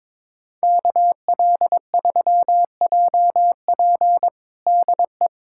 Morse code, heard as a keyed tone, about 22 wpm, 700 Hz. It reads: KL3JP DE
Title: USA Callsign DE Callsign - 5 Characters - 22wpm